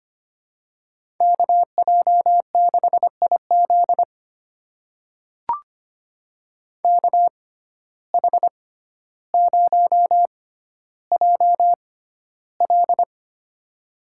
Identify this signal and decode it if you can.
Morse code, heard as a keyed tone, about 25 wpm, spaced as for 13 wpm, 700 Hz. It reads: KJ6IZ KH0JL
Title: USA Call Signs - 5 Characters - 6x Character Spacing - 25wpm